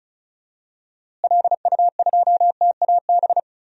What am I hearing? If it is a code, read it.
Morse code, heard as a keyed tone, about 36 wpm, 700 Hz. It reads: LU2TAB
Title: DXCC-100 Call Signs - 6 Characters - Repeated 2x - 35wpm